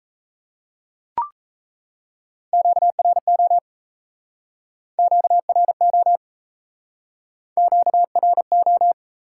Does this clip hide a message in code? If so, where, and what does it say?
Yes: Morse code from 2.5 to 8.9 s, heard as a keyed tone, about 36 wpm, 700 Hz. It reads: QRO QRO QRO